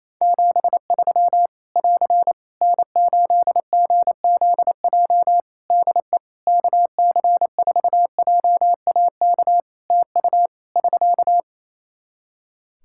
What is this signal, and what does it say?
Morse code, heard as a keyed tone, about 28 wpm, 700 Hz. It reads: 73 <AR> N8GZJ DE KC4JAK TU <SK>
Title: Practice QSO - Exchange 4 - Ending - 28wpm